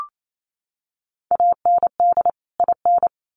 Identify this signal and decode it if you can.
Morse code, heard as a keyed tone, about 28 wpm, 700 Hz. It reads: AND IN
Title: Sets of 2 Words 28wpm